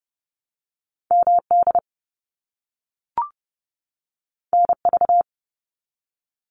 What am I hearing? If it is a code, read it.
Morse code, heard as a keyed tone, about 30 wpm, 700 Hz. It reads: MD NV
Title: US States 30wpm